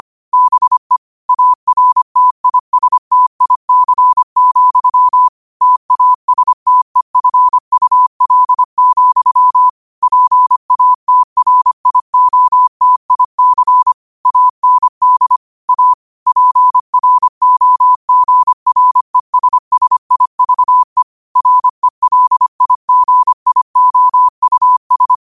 Morse code, heard as a keyed tone, about 25 wpm, 1000 Hz. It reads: BE ARTISTIC, TASTEFUL, PATRIOTIC AND A PROGRESSIVE RELIGIOUS